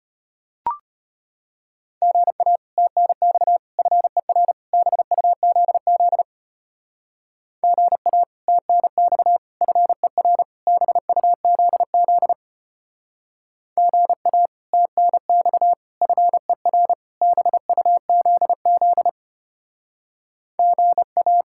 Morse code, heard as a keyed tone, about 33 wpm, 700 Hz. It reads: GA TNX FER BUZZ GA TNX FER BUZZ GA TNX FER BUZZ GA